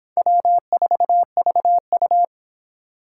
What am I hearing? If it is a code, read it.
Morse code, heard as a keyed tone, about 31 wpm, 700 Hz. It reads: W4VU